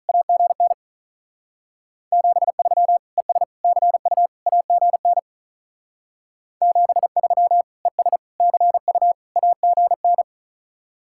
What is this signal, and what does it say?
Morse code, heard as a keyed tone, about 44 wpm, 700 Hz. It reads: AGN 73 ES CU AGN 73 ES CU AGN
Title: Speed-Racing Common QSO Phrases - 35wpm